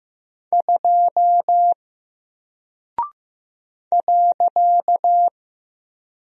Morse code, heard as a keyed tone, about 15 wpm, 700 Hz. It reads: 2 .
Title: Single Character (Letter:Number:Punctuation:Prosign) Alt 15wpm